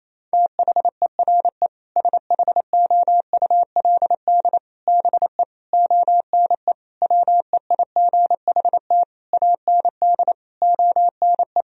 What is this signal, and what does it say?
Morse code, heard as a keyed tone, about 28 wpm, 700 Hz. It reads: THERE SHOULD BE ONE WEIGHT AND ONE
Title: Sentences from Top 1000 Words - Review Part 3 - 28wpm